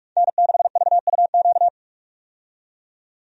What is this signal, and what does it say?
Morse code, heard as a keyed tone, about 46 wpm, 700 Hz. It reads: N6VUQ
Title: USA Call Signs - 5 Characters - 45wpm